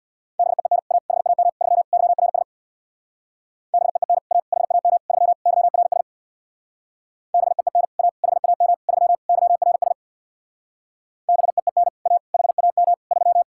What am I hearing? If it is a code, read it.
Morse code, heard as a keyed tone, about 64 wpm, 700 Hz. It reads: BEEN A HAM 3 YRS BEEN A HAM 3 YRS BEEN A HAM 3 YRS BEEN A HAM 3